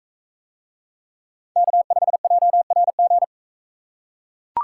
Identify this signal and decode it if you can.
Morse code, heard as a keyed tone, about 42 wpm, 700 Hz. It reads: K5JRG